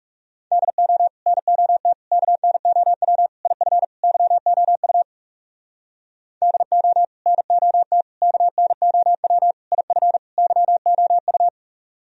Morse code, heard as a keyed tone, about 36 wpm, 700 Hz. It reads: DO NOT KNOW IF YOU DO NOT KNOW IF YOU
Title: Speed-Racing - Sets of 5 Words 30wpm